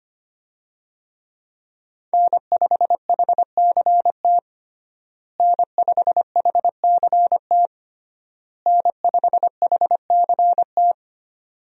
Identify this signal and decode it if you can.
Morse code, heard as a keyed tone, about 25 wpm, 700 Hz. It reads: N5HCT N5HCT N5HCT
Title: USA Call Signs - 5 Characters - Repeated 3x - 25wpm